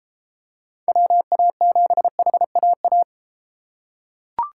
Morse code, heard as a keyed tone, about 31 wpm, 700 Hz. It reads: WA7HAA